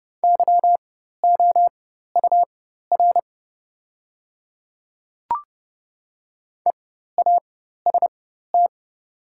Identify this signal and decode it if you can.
Morse code, heard as a keyed tone, about 30 wpm, spaced as for 14 wpm, 700 Hz. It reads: YOUR EAST
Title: 4x Character Spacing - Top 500 Words - 30wpm